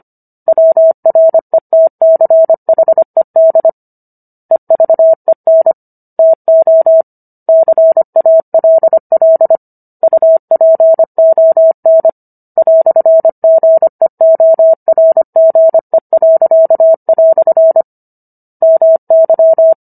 Morse code, heard as a keyed tone, about 25 wpm, 650 Hz. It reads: WRETCHED EVEN TO CALL UPON "GEORGE." MY